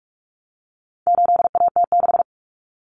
Morse code, heard as a keyed tone, about 44 wpm, 700 Hz. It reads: 8AT6
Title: Common Cut Numbers - 4-Digit 45wpm